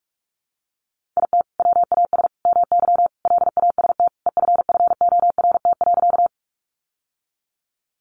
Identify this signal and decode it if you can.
Morse code, heard as a keyed tone, about 44 wpm, 700 Hz. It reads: IT WAS MY LAST EFFORT.
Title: Sentences from Top 1200 Words 45wpm